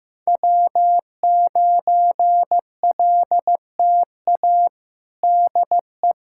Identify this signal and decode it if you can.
Morse code, heard as a keyed tone, about 16 wpm, 700 Hz. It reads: W9LTA DE
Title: USA Callsign DE Callsign - 5 Characters - 15wpm